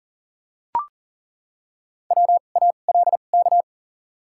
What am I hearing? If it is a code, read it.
Morse code, heard as a keyed tone, about 40 wpm, 700 Hz. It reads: W A L K
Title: 2x Character Spacing - Top 300 Words - 40wpm